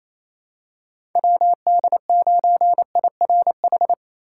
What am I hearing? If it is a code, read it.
Morse code, heard as a keyed tone, about 28 wpm, 700 Hz. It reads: WD9IRH
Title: USA Call Signs - 6 Characters - 3x Character Spacing - 28wpm